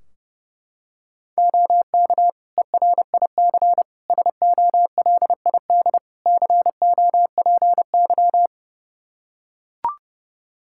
Morse code, heard as a keyed tone, about 32 wpm, 700 Hz. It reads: OK ERIC SOLID COPY
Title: Practice QSO - Exchange 3 - More Info - 30wpm